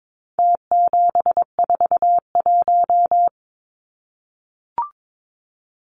Morse code, heard as a keyed tone, about 22 wpm, 700 Hz. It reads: T741
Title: Numbers - Contest Style - 4-Digit 22wpm